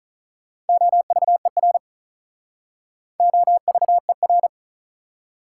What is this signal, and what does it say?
Morse code, heard as a keyed tone, about 45 wpm, 690 Hz. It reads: OVER OVER